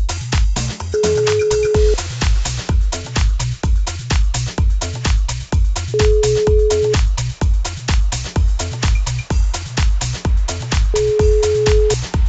Ringback tone (Europe, 425 Hz).